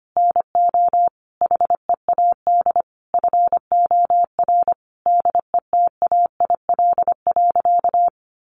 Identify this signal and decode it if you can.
Morse code, heard as a keyed tone, about 25 wpm, 700 Hz. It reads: NO HEAD FOR DETAIL.
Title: Sentences from Top 1500 Words 25wpm